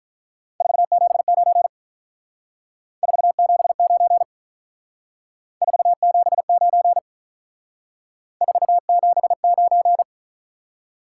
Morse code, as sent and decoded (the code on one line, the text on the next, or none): ....- --... ----. / ....- --... ----. / ....- --... ----. / ....- --... ----.
479 479 479 479